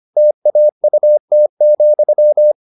Morse code, heard as a keyed tone, about 25 wpm, 600 Hz. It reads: TAUT,